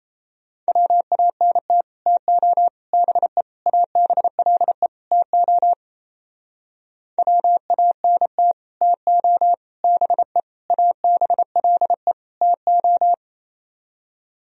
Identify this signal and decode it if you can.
Morse code, heard as a keyed tone, about 38 wpm, 700 Hz. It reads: WANT TO BE ABLE TO WANT TO BE ABLE TO